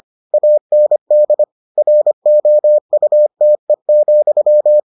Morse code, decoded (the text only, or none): AND ROUTE,